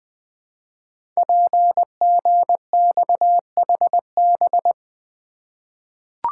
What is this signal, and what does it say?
Morse code, heard as a keyed tone, about 20 wpm, 700 Hz. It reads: PGXHB
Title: Code Groups - 5 Letters - 20wpm